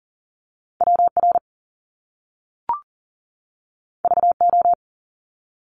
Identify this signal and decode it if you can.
Morse code, heard as a keyed tone, about 40 wpm, 700 Hz. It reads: WR VO
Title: Top Two Letter Combinations in Words 40wpm